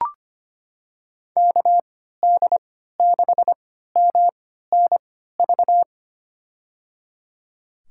Morse code, heard as a keyed tone, about 25 wpm, spaced as for 14 wpm, 700 Hz. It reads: KD6MNV
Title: USA Call Signs - 6 Characters - 3x Character Spacing - 25wpm